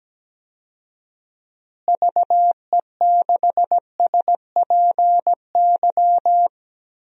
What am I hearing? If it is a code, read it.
Morse code, heard as a keyed tone, about 17 wpm, 700 Hz. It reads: VE6SPY
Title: DXCC-100 Call Signs - 6 Characters - Repeated 2x - 17wpm